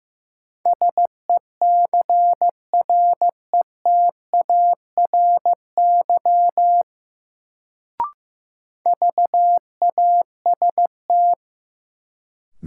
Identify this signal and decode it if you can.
Morse code, heard as a keyed tone, about 16 wpm, 700 Hz. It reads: SECRETARY VAST
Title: Sentences from Top 1300 Words - New Word Drill Alt - 15wpm